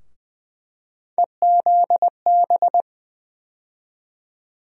Morse code, heard as a keyed tone, about 21 wpm, 700 Hz. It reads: EZB
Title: USA Call Sign Postfixes - 3 Letters 20wpm